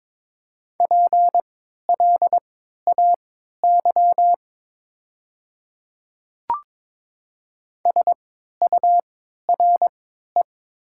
Morse code, heard as a keyed tone, about 22 wpm, spaced as for 13 wpm, 700 Hz. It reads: PLAY SURE
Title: 3x Character Spacing - Top 300 Words - 22wpm